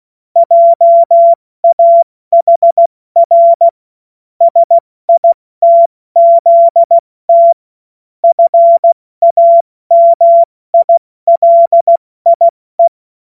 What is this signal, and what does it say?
Morse code, heard as a keyed tone, about 16 wpm, 680 Hz. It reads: JAHR SITZT FAMILIE
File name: Morsekurs / Witze / Witz 24. Januar 2023 70 F80